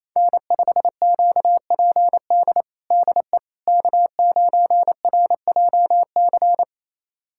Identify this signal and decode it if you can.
Morse code, heard as a keyed tone, about 28 wpm, 700 Hz. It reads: N5QPD DE K9RJC
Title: USA Callsign DE Callsign - 5 Characters - 28wpm